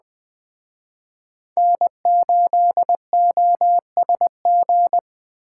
Morse code, heard as a keyed tone, about 20 wpm, 700 Hz. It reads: N8OSG